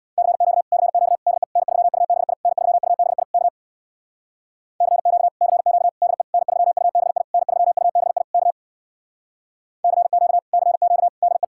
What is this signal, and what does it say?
Morse code, heard as a keyed tone, about 61 wpm, 700 Hz. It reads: CQ CQ DE N3UBI N3UBI K CQ CQ DE N3UBI N3UBI K CQ CQ DE
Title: Speed-Racing Common QSO Phrases Encore - 50wpm